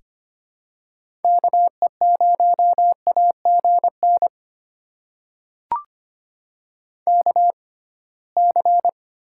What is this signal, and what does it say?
Morse code, heard as a keyed tone, about 25 wpm, spaced as for 19 wpm, 700 Hz. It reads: KE0AGN KC
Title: USA Call Signs - 6 Characters - 6x Character Spacing - 25wpm